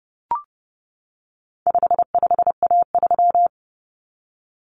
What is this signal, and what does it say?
Morse code, heard as a keyed tone, about 30 wpm, 700 Hz. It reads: 55A3
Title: Common Cut Numbers - 4-Digit 30wpm